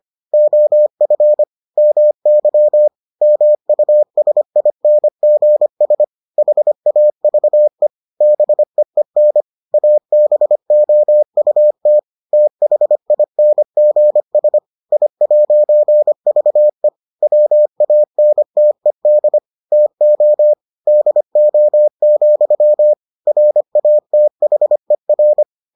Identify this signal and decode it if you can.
Morse code, heard as a keyed tone, about 25 wpm, 600 Hz. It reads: OF MY MUSINGS HAVE BEEN ABOUT THINGS I'VE WANTED TO DO, RATHER